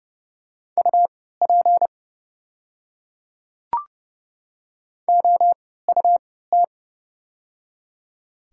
Morse code, heard as a keyed tone, about 30 wpm, spaced as for 17 wpm, 690 Hz. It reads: UP OUT
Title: 3x Character Spacing - Top 500 Words - 30wpm